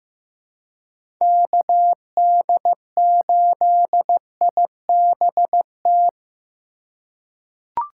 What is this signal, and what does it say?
Morse code, heard as a keyed tone, about 15 wpm, 700 Hz. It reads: KD8IBT
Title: USA Call Signs - 6 Characters - 2x Character Spacing - 15wpm